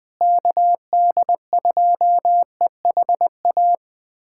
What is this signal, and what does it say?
Morse code, heard as a keyed tone, about 21 wpm, 700 Hz. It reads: KD2EHA
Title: USA Callsign DE Callsign - 6 Characters - 20wpm